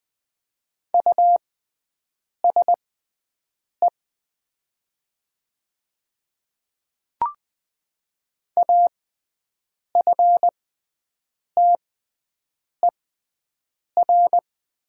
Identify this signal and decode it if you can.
Morse code, heard as a keyed tone, about 20 wpm, spaced as for 7 wpm, 700 Hz. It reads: USE AFTER
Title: 6x Character Spacing - Top 300 Words - 20wpm